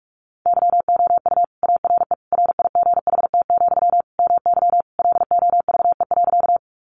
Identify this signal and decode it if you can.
Morse code, heard as a keyed tone, about 46 wpm, 700 Hz. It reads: YOU ARE RIGHT, MY LOVE.
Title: Sentences from Top 400 Words 45wpm